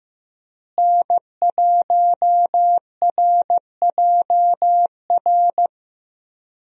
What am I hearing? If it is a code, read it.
Morse code, heard as a keyed tone, about 16 wpm, 700 Hz. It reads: N1RJR
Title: USA Call Signs - 5 Characters - Repeated 3x - 15wpm